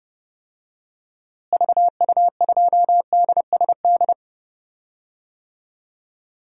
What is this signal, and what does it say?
Morse code, heard as a keyed tone, about 32 wpm, 700 Hz. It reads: VU2DSD
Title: DXCC-100 Call Signs - 6 Characters - Repeated 2x - 30wpm